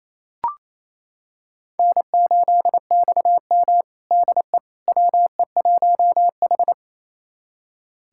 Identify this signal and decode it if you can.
Morse code, heard as a keyed tone, about 28 wpm, 700 Hz. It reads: N8XM DE WE1H
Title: USA Callsign DE Callsign - 4 Characters - 28wpm